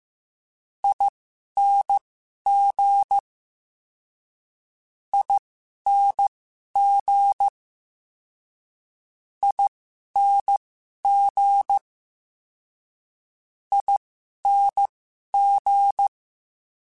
Morse code, heard as a keyed tone, about 15 wpm, spaced as for 11 wpm, 800 Hz. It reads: ING ING ING ING
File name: Special INGs-mp3s / Special INGs-mp3s / Ere ing @10